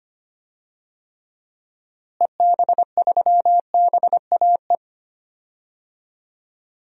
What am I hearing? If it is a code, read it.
Morse code, heard as a keyed tone, about 25 wpm, 700 Hz. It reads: EB3BAE
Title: DXCC-100 Call Signs - 6 Characters - Repeated 3x - 25wpm